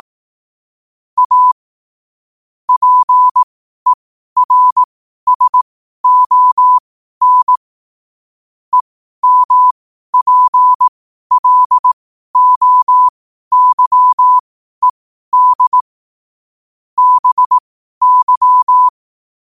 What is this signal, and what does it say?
Morse code, heard as a keyed tone, about 18 wpm, spaced as for 12 wpm, 1000 Hz. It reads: A PERSON EMPLOYED BY
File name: qotdfiles / 12wpm